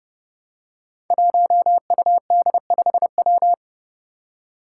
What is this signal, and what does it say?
Morse code, heard as a keyed tone, about 30 wpm, 700 Hz. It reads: 1UD5W
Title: Five Challenging Letter-Number Combinations 30wpm